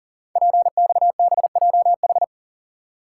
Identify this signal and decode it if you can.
Morse code, heard as a keyed tone, about 42 wpm, 700 Hz. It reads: PXBJH
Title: Code Groups - 5 Letters - 40wpm